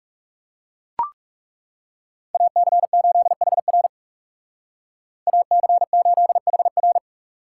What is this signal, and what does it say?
Morse code, heard as a keyed tone, about 37 wpm, 700 Hz. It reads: AC8HR AC8HR
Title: Speed Racing - USA 2x2 Call Signs 30wpm